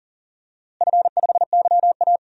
Morse code, heard as a keyed tone, about 40 wpm, 700 Hz. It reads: F5YA
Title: DXCC-100 Call Signs - 4 Characters - Repeated 3x - 40wpm